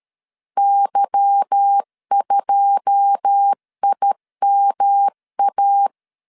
Morse code, heard as a keyed tone, about 13 wpm, 800 Hz. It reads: Y2IMA
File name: LEZIONI / 9 28 12wpm 2351